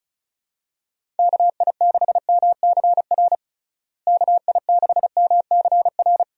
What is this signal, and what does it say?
Morse code, heard as a keyed tone, about 36 wpm, 700 Hz. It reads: KI6MCR KI6MCR
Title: USA Call Signs - 6 Characters - Repeated 2x - 35wpm